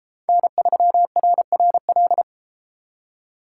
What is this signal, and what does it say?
Morse code, heard as a keyed tone, about 31 wpm, 700 Hz. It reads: N3RRL